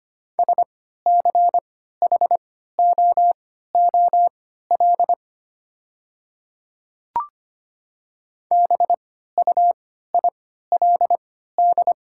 Morse code, heard as a keyed tone, about 25 wpm, spaced as for 14 wpm, 700 Hz. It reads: SCHOOL BUILD
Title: 3x Character Spacing - Top 300 Words - 25wpm